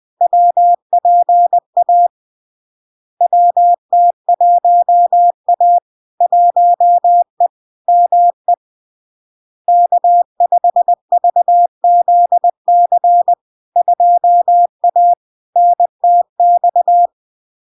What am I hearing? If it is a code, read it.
Morse code, heard as a keyed tone, about 20 wpm, 700 Hz. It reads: WPA WT1A 1E ME K5VZC 2A NTX
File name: LARK-FD-2022-Log-in-CW-20wpm-700-Hz